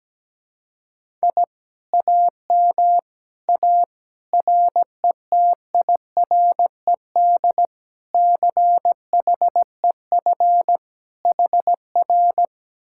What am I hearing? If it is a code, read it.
Morse code, heard as a keyed tone, about 17 wpm, 700 Hz. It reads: I AM A RETIRED CHEF HR
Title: Speed-Racing Common QSO Phrases - 17wpm